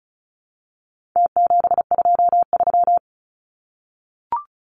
Morse code, heard as a keyed tone, about 34 wpm, 700 Hz. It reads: T723
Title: Numbers - Contest Style - 4-Digit 35wpm